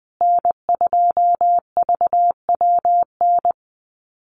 Morse code, heard as a keyed tone, about 20 wpm, 700 Hz. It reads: N2VWN
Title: USA Call Signs Repeated 3x 20wpm